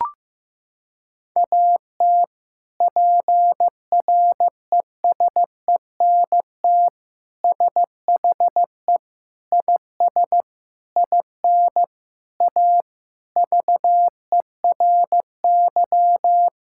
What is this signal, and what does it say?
Morse code, heard as a keyed tone, about 15 wpm, 700 Hz. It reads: AT PRESENT SHE IS IN A VERY
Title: Sentences from Top 1000 Words - Review Part 5 - 15wpm